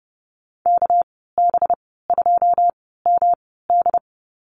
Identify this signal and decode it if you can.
Morse code, heard as a keyed tone, about 30 wpm, spaced as for 17 wpm, 700 Hz. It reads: KB2MD
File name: Lesson 37g - Letter x - Call Signs 30wpm